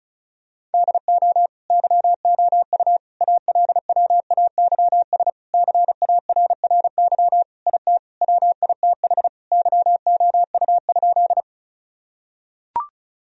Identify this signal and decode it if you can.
Morse code, heard as a keyed tone, about 34 wpm, 700 Hz. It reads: DO YOU ALWAYS CARRY IT WITH YOU?
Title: Sentences from Top 1000 Words - Review Part 1 - 35wpm